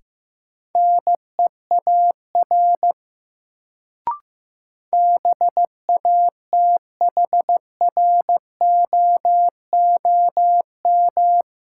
Morse code, heard as a keyed tone, about 15 wpm, 700 Hz. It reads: NEAR BATHROOM
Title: Sets of 5 Words Encore - New Word Drill - 15wpm